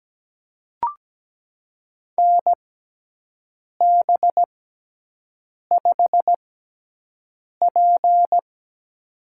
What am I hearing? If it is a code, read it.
Morse code, heard as a keyed tone, about 17 wpm, spaced as for 6 wpm, 700 Hz. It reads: NB5P
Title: USA Call Signs - 6x Character Spacing - 17wpm